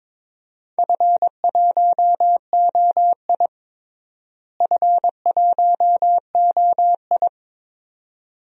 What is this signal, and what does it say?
Morse code, heard as a keyed tone, about 22 wpm, 700 Hz. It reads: F1OI F1OI
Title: DXCC-100 Call Signs - 4 Characters - Repeated 2x - 22wpm